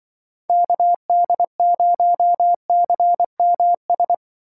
Morse code, heard as a keyed tone, about 24 wpm, 700 Hz. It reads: KD0CMS
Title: Speed Racing - USA Call Signs 20wpm